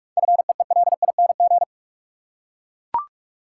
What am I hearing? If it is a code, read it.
Morse code, heard as a keyed tone, about 41 wpm, 700 Hz. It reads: FEELING
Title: Seven Letter Words - 45wpm